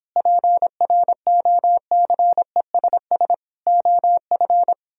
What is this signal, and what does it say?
Morse code, heard as a keyed tone, about 31 wpm, 700 Hz. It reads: PROCESS OF